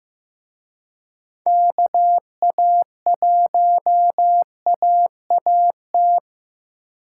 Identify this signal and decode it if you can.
Morse code, heard as a keyed tone, about 18 wpm, 700 Hz. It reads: KA1AAT
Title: Speed Racing - USA 2x3 Call Signs 15wpm